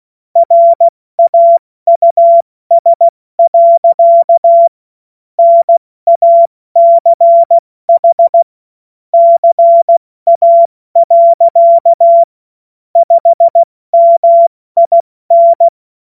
Morse code, heard as a keyed tone, about 16 wpm, 680 Hz. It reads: RAUS. NACH CA. 5MIN
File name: Morsekurs / Witze / Witz 02 Januar 2024 70 F80